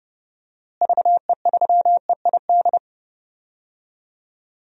Morse code, heard as a keyed tone, about 29 wpm, 700 Hz. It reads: VE3EID
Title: DXCC-100 Call Signs - 6 Characters - Repeated 3x - 30wpm